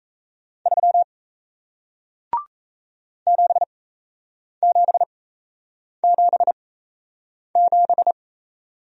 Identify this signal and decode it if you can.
Morse code, heard as a keyed tone, about 38 wpm, spaced as for 8 wpm, 700 Hz. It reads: 3 7777